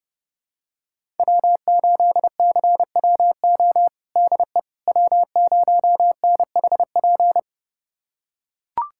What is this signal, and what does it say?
Morse code, heard as a keyed tone, about 30 wpm, 700 Hz. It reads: W8CWO DE W0NHP
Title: USA Callsign DE Callsign - 5 Characters - 30wpm